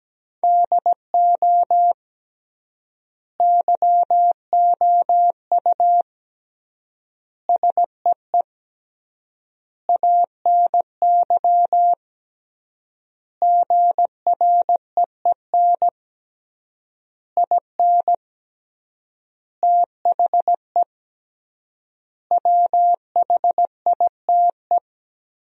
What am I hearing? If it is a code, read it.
Morse code, heard as a keyed tone, about 17 wpm, 700 Hz. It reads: DO YOU SEE ANY GREEN IN THE WHITE
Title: 3x Word-Spacing - Sentences from Top 500 Words - 17wpm